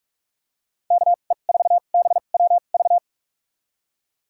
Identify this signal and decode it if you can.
Morse code, heard as a keyed tone, about 44 wpm, spaced as for 32 wpm, 700 Hz. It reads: KE4BWV
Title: USA Call Signs - 6 Characters - 2x Character Spacing - 45wpm